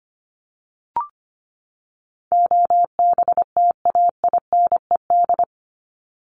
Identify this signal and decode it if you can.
Morse code, heard as a keyed tone, about 25 wpm, 700 Hz. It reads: OBTAINED
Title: ED Suffix - Top 100 Words with Base Words 25wpm